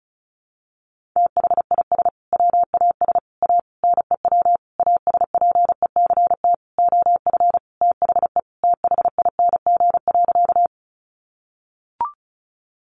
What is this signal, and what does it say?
Morse code, heard as a keyed tone, about 34 wpm, 700 Hz. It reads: THIS WAS A NEW ASPECT OF THE THING.
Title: Sentences from Top 1300 Words 35wpm